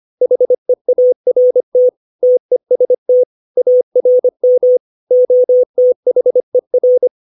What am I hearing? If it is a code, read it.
Morse code, heard as a keyed tone, about 25 wpm, 500 Hz. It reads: HEART TEST ARM OTHER
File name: 632 words / 632 Words 25WPM